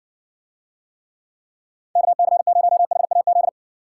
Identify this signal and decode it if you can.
Morse code, heard as a keyed tone, about 60 wpm, 700 Hz. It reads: KC0HAZ